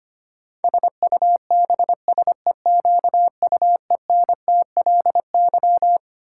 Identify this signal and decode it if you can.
Morse code, heard as a keyed tone, about 25 wpm, 700 Hz. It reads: SUBSEQUENTLY
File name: Twelve Letter Words - 25wpm